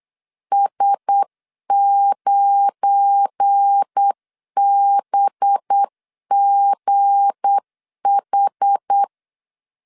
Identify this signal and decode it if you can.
Morse code, heard as a keyed tone, about 8 wpm, 800 Hz. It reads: S9BGH
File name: LEZIONI / 9 28 8wpm 3512